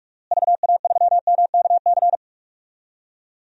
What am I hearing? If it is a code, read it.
Morse code, heard as a keyed tone, about 46 wpm, 700 Hz. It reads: VA3MKC
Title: DXCC-100 Call Signs - 6 Characters - Repeated 3x - 45wpm